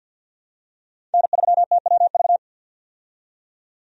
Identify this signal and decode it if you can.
Morse code, heard as a keyed tone, about 47 wpm, 700 Hz. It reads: N3TWV
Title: USA Call Signs - 5 Characters - 50wpm